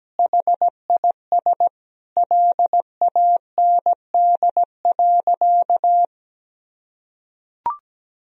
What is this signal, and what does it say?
Morse code, heard as a keyed tone, about 17 wpm, 700 Hz. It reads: HIS LAND.
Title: Sentences from Top 1000 Words - Review Part 1 - 17wpm